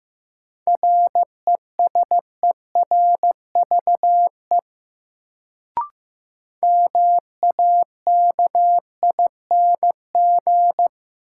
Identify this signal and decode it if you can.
Morse code, heard as a keyed tone, about 16 wpm, 700 Hz. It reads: RESERVE MAKING
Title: Sentences from Top 1000 Words - Review Part 6 - New Word Drill - 15wpm